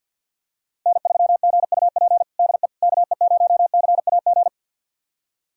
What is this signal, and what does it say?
Morse code, heard as a keyed tone, about 50 wpm, 700 Hz. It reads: N3GUP DE KE0CAG
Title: USA Callsign DE Callsign 50wpm